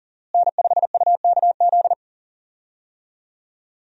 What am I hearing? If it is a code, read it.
Morse code, heard as a keyed tone, about 40 wpm, 700 Hz. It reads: N5UKZ
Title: USA Call Signs - 5 Characters - Repeated 2x - 40wpm